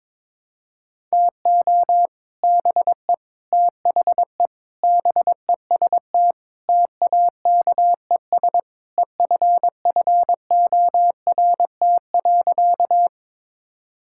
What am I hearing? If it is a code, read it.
Morse code, heard as a keyed tone, about 21 wpm, 700 Hz. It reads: TO BE THE BEST TAKES EFFORT.
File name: practice / 22wpm / Sentences with 3-Word Phrases - 22wpm